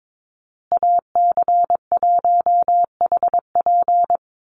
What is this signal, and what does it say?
Morse code, heard as a keyed tone, about 22 wpm, 700 Hz. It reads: AC1HP
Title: USA Call Signs Repeated 2x 22wpm